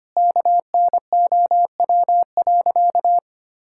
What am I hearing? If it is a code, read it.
Morse code, heard as a keyed tone, about 25 wpm, 700 Hz. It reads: KNOW.
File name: practice / 25wpm / Sentences with 3-Word Phrases - 25wpm